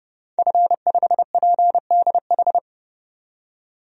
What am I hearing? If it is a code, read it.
Morse code, heard as a keyed tone, about 32 wpm, 700 Hz. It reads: F5PDH
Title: DXCC-100 Call Signs - 5 Characters - 30wpm